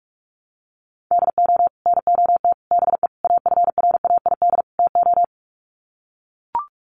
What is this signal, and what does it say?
Morse code, heard as a keyed tone, about 44 wpm, 700 Hz. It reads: DO NOT BE AFRAID TO
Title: Sets of 5 Words Encore2 45wpm